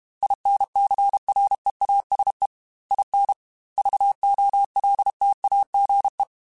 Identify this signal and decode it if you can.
Morse code, heard as a keyed tone, about 32 wpm, 800 Hz. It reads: INCREASE IN VOLTAGE
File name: Technical Subjects / Technical Subjects-mp3s / XC @30